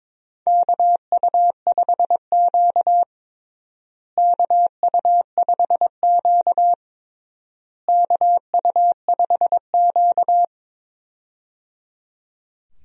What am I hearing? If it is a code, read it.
Morse code, heard as a keyed tone, about 22 wpm, 700 Hz. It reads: KU5Q KU5Q KU5Q
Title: USA Call Signs - 4 Characters - Repeated 3x - 22wpm